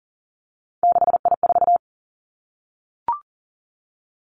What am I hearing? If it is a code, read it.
Morse code, heard as a keyed tone, about 42 wpm, 700 Hz. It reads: 6I4
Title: Three Challenging Letter-Number Combinations 40wpm